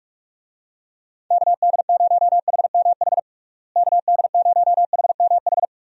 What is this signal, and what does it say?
Morse code, heard as a keyed tone, about 46 wpm, 700 Hz. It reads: KD0HMH KD0HMH
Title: USA Call Signs - 6 Characters - Repeated 2x - 45wpm